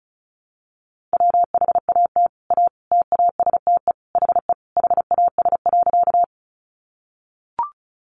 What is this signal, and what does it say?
Morse code, heard as a keyed tone, about 34 wpm, 700 Hz. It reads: WHAT A TASTE HE HAS.
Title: Sentences from Top 1100 Words 35wpm